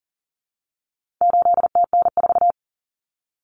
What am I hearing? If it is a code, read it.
Morse code, heard as a keyed tone, about 40 wpm, 700 Hz. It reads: 8TN4
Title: Common Cut Numbers - 4-Digit 40wpm